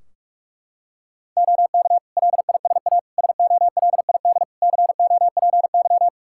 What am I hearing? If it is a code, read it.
Morse code, heard as a keyed tone, about 42 wpm, 700 Hz. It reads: OK LISA SOLID COPY
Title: Practice QSO - Exchange 3 - More Info - 45wpm